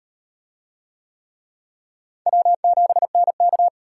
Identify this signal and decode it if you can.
Morse code, heard as a keyed tone, about 34 wpm, 700 Hz. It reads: W7NK